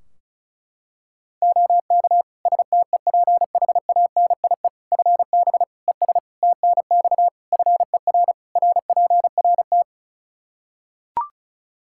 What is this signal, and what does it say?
Morse code, heard as a keyed tone, about 34 wpm, 700 Hz. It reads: OK STEPHANIE FB ES TNX FER RPRT
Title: Practice QSO - Exchange 2 - Info - 35wpm